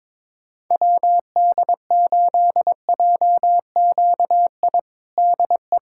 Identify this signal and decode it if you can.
Morse code, heard as a keyed tone, about 22 wpm, 700 Hz. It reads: WD8JQI DE
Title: USA Callsign DE Callsign 22wpm